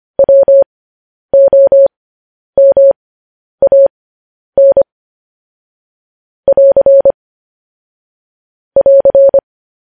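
Morse code, heard as a keyed tone, about 25 wpm, spaced as for 10 wpm, 550 Hz. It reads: WOMAN <AR> <AR>
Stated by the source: Morse code news headlines at 10 wpm.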